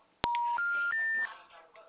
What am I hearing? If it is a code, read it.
Special information tone.
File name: out- -2009- - - .119